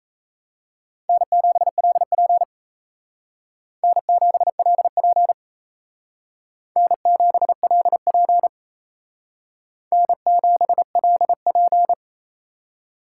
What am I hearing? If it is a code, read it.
Morse code, heard as a keyed tone, about 35 wpm, 700 Hz. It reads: N7LP N7LP N7LP N7LP